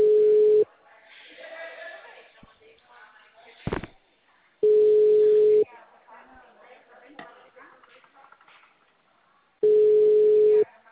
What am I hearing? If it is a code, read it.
Ringback tone (Europe, 425 Hz), with speech.